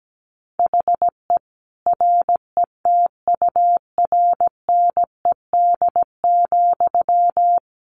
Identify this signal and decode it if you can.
Morse code, heard as a keyed tone, about 17 wpm, 700 Hz. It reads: HE RETURNED,
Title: A Sound Of Thunder 17wpm